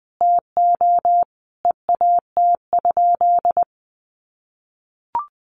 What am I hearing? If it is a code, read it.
Morse code, heard as a keyed tone, about 20 wpm, 700 Hz. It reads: TO EAT?
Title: Sentences from Top 300 Words 20wpm